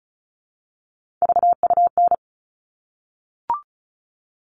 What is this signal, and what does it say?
Morse code, heard as a keyed tone, about 34 wpm, 700 Hz. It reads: VUN
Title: All Cut Numbers - 3-Digit 35wpm